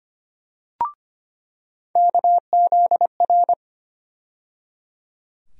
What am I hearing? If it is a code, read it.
Morse code, heard as a keyed tone, about 25 wpm, 700 Hz. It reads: KZR
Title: USA Call Sign Postfixes - 3 Letters 25wpm